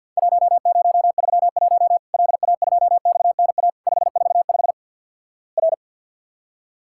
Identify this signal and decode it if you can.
Morse code, heard as a keyed tone, about 50 wpm, 700 Hz. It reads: 1031 LA2XNA 545 R
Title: Worked All Europe DX Contest - QTC Exchange - 50wpm